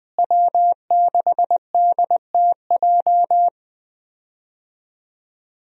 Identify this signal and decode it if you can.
Morse code, heard as a keyed tone, about 20 wpm, 700 Hz. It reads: W6DTJ
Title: USA Call Signs - 5 Characters - Repeated 2x - 20wpm